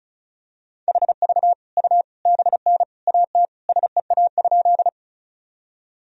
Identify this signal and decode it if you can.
Morse code, heard as a keyed tone, about 36 wpm, 700 Hz. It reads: HV U BN AT SEA?
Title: Sentences from Top 300 Words and Abbreviations 35wpm